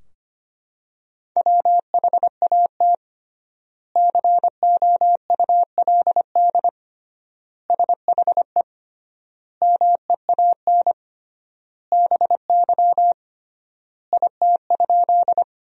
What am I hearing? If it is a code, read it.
Morse code, heard as a keyed tone, about 25 wpm, 700 Hz. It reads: WHAT COULD SHE MEAN BY IT?
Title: 3x Word-Spacing - Sentences from Top 200 Words - 25wpm